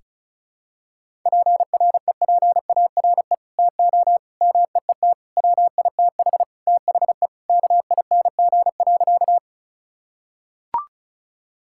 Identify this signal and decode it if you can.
Morse code, heard as a keyed tone, about 36 wpm, 700 Hz. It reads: PREPARE TO MEET WITH THE KING.
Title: Sentences from Top 1000 Words - Review Part 6 - 35wpm